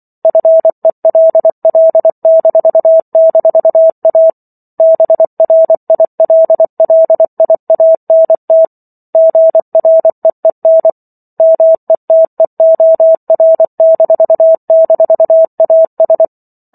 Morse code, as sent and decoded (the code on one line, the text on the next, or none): ..-. .-.. .-.. -....- -....- .- / -... .-. .. .-.. .-.. .. .- -. - / --. .-. . -. / -- . - . --- .-. -....- -....- .- ...
FELL--A BRILLIANT GREEN METEOR--AS